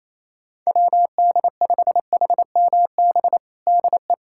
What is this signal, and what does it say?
Morse code, heard as a keyed tone, about 28 wpm, 700 Hz. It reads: WD5HMB DE
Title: USA Callsign DE Callsign 28wpm